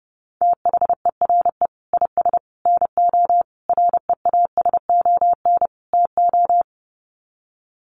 Morse code, heard as a keyed tone, about 30 wpm, 700 Hz. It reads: THERE IS NO REASON TO
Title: Sets of 5 Words Encore 30wpm